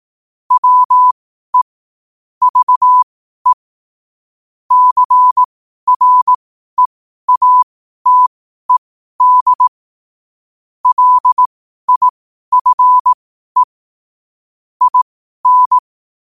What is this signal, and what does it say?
Morse code, heard as a keyed tone, about 18 wpm, spaced as for 12 wpm, 1000 Hz. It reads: WE VE CREATED LIFE IN
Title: 12 WPM morse code quotes for Fri, 01 Mar 2024